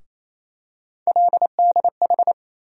Morse code, heard as a keyed tone, about 28 wpm, 700 Hz. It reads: LDH
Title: USA Call Sign Postfixes - 3 Letters 28wpm